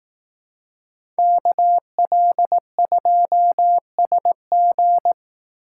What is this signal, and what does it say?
Morse code, heard as a keyed tone, about 18 wpm, 700 Hz. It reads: KL2SG
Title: Speed Racing - USA Call Signs 15wpm